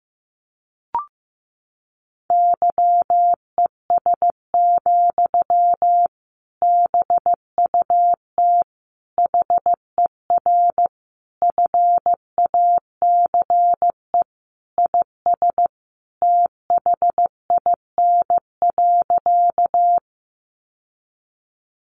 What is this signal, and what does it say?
Morse code, heard as a keyed tone, about 15 wpm, 700 Hz. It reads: YES, BUT HER FACE IS THIN.
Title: Sentences from Top 1200 Words 15wpm